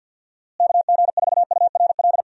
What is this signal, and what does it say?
Morse code, heard as a keyed tone, about 52 wpm, 700 Hz. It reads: KG4URL
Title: USA Call Signs - 6 Characters - Repeated 2x - 50wpm